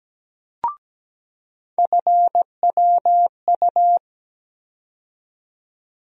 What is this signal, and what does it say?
Morse code, heard as a keyed tone, about 17 wpm, 700 Hz. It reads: FWU
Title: USA Call Sign Postfixes - 3 Letters 17wpm